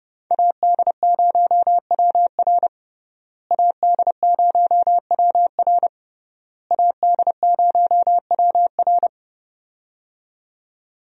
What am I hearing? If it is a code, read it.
Morse code, heard as a keyed tone, about 28 wpm, 700 Hz. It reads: AD0WR AD0WR AD0WR
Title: USA Call Signs - 5 Characters - Repeated 3x - 30wpm